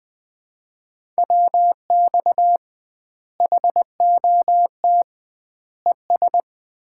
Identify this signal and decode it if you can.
Morse code, heard as a keyed tone, about 20 wpm, 700 Hz. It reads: WX HOT ES
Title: Common QSO Phrases - 2x Word-Spacing - 20wpm